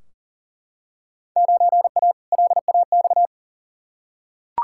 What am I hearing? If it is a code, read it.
Morse code, heard as a keyed tone, about 40 wpm, 700 Hz. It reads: 9A LAX
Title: Field Day Exchanges Encore - 40wpm